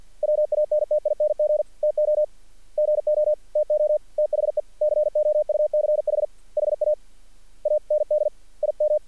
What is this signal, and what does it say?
Morse code, heard as a keyed tone, about 48 wpm, 600 Hz. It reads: WANTING TO GO TO THE COUCH HA AND IM
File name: archived / CRSnet